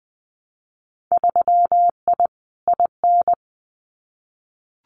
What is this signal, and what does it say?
Morse code, heard as a keyed tone, about 19 wpm, 700 Hz. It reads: 3I IN
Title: Winter Field Day Exchanges 20wpm